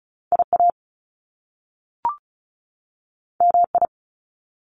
Morse code, heard as a keyed tone, about 36 wpm, 700 Hz. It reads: IA MI
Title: Top Two Letter Combinations in Words 35wpm